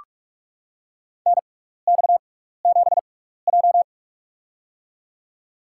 Morse code, heard as a keyed tone, about 44 wpm, spaced as for 15 wpm, 690 Hz. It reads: NX7J
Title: USA Call Signs - 6x Character Spacing - 45wpm